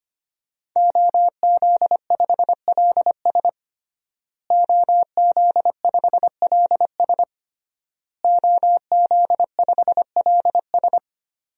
Morse code, heard as a keyed tone, about 25 wpm, 700 Hz. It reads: OZ5LS OZ5LS OZ5LS
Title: DXCC-100 Call Signs - 5 Characters - Repeated 3x - 25wpm